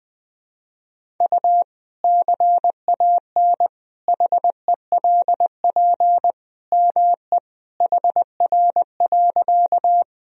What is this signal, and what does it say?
Morse code, heard as a keyed tone, about 20 wpm, 700 Hz. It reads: U CAN HELP ME HR.
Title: Sentences from Top 200 Words and Abbreviations 20wpm